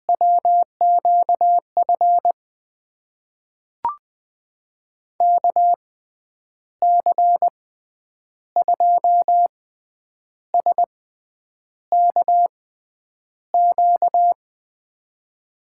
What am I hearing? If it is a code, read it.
Morse code, heard as a keyed tone, about 21 wpm, spaced as for 9 wpm, 700 Hz. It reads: WQF KC2SKQ
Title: USA Call Signs - 6 Characters - 6x Character Spacing - 20wpm